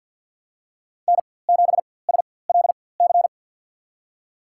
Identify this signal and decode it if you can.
Morse code, heard as a keyed tone, about 47 wpm, spaced as for 23 wpm, 700 Hz. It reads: N7SLC
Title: USA Call Signs - 4x Character Spacing - 50wpm